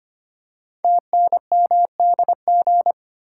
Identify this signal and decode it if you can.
Morse code, heard as a keyed tone, about 25 wpm, 700 Hz. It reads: TNMDG
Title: Code Groups - 5 Letters - 25wpm